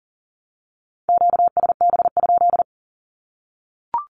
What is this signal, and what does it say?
Morse code, heard as a keyed tone, about 40 wpm, 700 Hz. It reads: QSB?
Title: Q-Codes Repeated 3x 40wpm